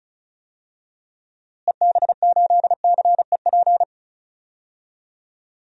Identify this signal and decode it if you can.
Morse code, heard as a keyed tone, about 36 wpm, 700 Hz. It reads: EB8CEP
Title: DXCC-100 Call Signs - 6 Characters - Repeated 3x - 35wpm